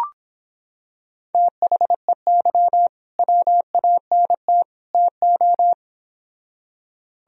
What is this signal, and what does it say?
Morse code, heard as a keyed tone, about 25 wpm, 700 Hz. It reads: THEY WANT TO